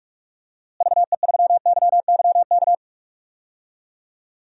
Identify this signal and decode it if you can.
Morse code, heard as a keyed tone, about 46 wpm, 700 Hz. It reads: VE3YYK
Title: DXCC-100 Call Signs - 6 Characters - Repeated 3x - 45wpm